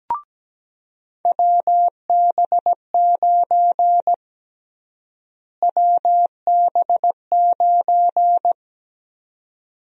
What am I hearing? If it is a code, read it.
Morse code, heard as a keyed tone, about 17 wpm, 700 Hz. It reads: WB9 WB9
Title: USA Call Sign Prefixes - Repeated 2x - 17wpm